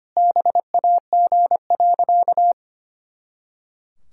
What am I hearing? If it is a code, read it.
Morse code, heard as a keyed tone, about 25 wpm, 700 Hz. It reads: BAG.
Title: Pangram Challenge v2 - 25wpm